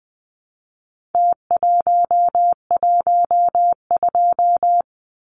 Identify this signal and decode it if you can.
Morse code, heard as a keyed tone, about 20 wpm, 700 Hz. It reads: T112
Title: Numbers - Contest Style - 4-Digit 20wpm